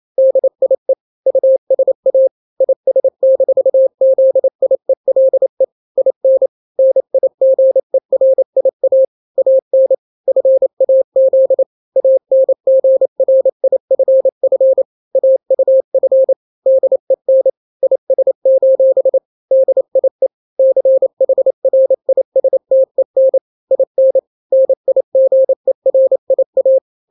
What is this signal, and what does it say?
Morse code, heard as a keyed tone, about 28 wpm, 530 Hz. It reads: DIE USA IS-ZIELE IN NIGERIA AN FAZ ANGRIFF AUF DEN IS: DIE CHRISTEN IN NIGERIA
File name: daily / 27wpm